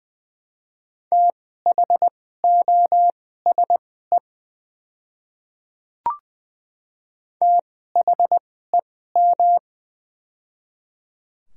Morse code, heard as a keyed tone, about 20 wpm, spaced as for 15 wpm, 700 Hz. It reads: THOSE THEM
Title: 2x Character Spacing - Top 300 Words - 20wpm